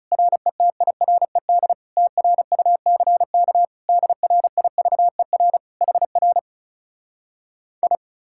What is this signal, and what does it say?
Morse code, heard as a keyed tone, about 40 wpm, 700 Hz. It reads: RETIRED TRUCK DRIVER HR I